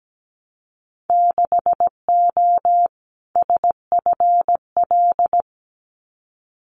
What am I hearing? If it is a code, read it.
Morse code, heard as a keyed tone, about 17 wpm, 700 Hz. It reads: 6O SFL
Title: Winter Field Day Exchanges Encore 17wpm